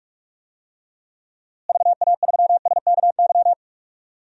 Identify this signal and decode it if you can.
Morse code, heard as a keyed tone, about 46 wpm, 700 Hz. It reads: VA3SKY
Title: DXCC-100 Call Signs - 6 Characters - Repeated 3x - 45wpm